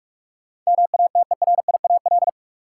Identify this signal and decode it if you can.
Morse code, heard as a keyed tone, about 46 wpm, 700 Hz. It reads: MATERIAL
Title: Sentences from Top 1000 Words - Review Part 5 - New Word Drill - 45wpm